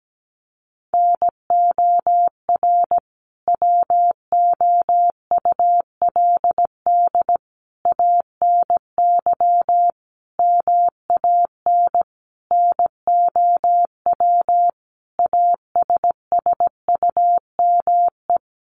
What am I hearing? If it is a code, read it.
Morse code, heard as a keyed tone, about 17 wpm, 700 Hz. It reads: NOR WOULD ANY MAN NOW ASSUME
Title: Sentences from Top 1500 Words 17wpm